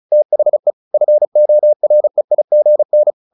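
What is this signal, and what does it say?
Morse code, heard as a keyed tone, about 34 wpm, 600 Hz. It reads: THE FOREIGN
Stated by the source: Morse practice at 35 wpm from CW med Gnister, sending Ken Follett's novel Never in English.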